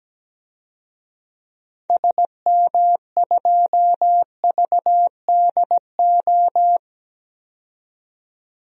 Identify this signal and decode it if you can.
Morse code, heard as a keyed tone, about 17 wpm, 700 Hz. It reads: SM2VDO
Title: DXCC-100 Call Signs - 6 Characters - 17wpm